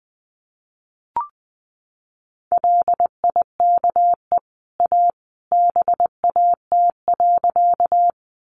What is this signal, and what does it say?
Morse code, heard as a keyed tone, about 20 wpm, 700 Hz. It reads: LIKE A BAT.
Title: Sentences from Top 1500 Words 20wpm